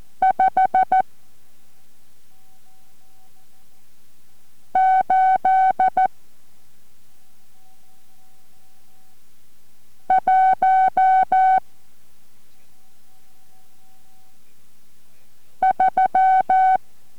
Morse code, heard as a keyed tone, about 14 wpm, spaced as for 2 wpm, 760 Hz. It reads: 5813